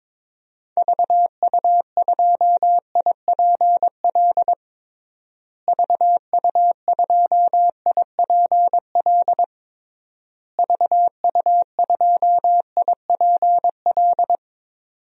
Morse code, heard as a keyed tone, about 22 wpm, 700 Hz. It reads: VU2IPL VU2IPL VU2IPL
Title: DXCC-100 Call Signs - 6 Characters - Repeated 3x - 22wpm